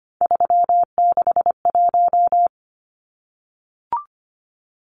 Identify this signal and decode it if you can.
Morse code, heard as a keyed tone, about 25 wpm, 700 Hz. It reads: 361
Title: Numbers - 3-Digit 25wpm